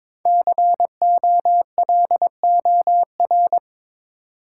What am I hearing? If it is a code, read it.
Morse code, heard as a keyed tone, about 22 wpm, 700 Hz. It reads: COLOR
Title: Sentences from Top 1000 Words - Review Part 2 - New Word Drill - 22wpm